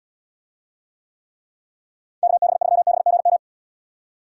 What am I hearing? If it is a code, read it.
Morse code, heard as a keyed tone, about 79 wpm, 700 Hz. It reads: KB3BGM